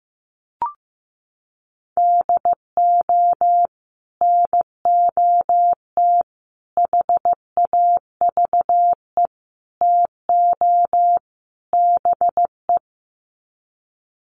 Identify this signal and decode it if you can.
Morse code, heard as a keyed tone, about 15 wpm, 700 Hz. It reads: DO NOT HAVE TO BE
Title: Sets of 5 Words 15wpm